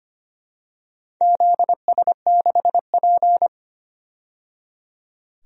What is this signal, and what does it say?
Morse code, heard as a keyed tone, about 25 wpm, 700 Hz. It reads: ZS6P
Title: DXCC-100 Call Signs - 4 Characters - 25wpm